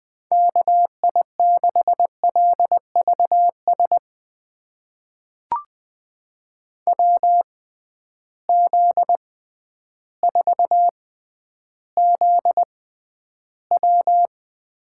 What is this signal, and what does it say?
Morse code, heard as a keyed tone, about 20 wpm, spaced as for 11 wpm, 700 Hz. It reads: KI6LVS WZ4ZW
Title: USA Call Signs - 6x Character Spacing - 20wpm